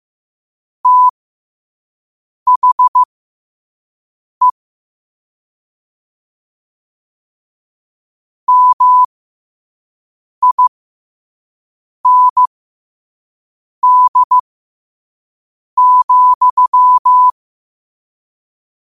Morse code, heard as a keyed tone, about 15 wpm, spaced as for 5 wpm, 1000 Hz. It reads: THE MIND,
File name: qotdfiles / 5wpm